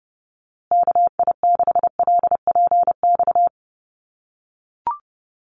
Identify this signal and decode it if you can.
Morse code, heard as a keyed tone, about 28 wpm, 700 Hz. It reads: KI6LPX
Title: USA Call Signs Repeated 3x 30wpm